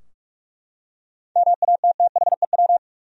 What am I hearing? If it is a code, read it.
Morse code, heard as a keyed tone, about 45 wpm, 700 Hz. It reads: MATTHEW
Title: USA Common Names - 45wpm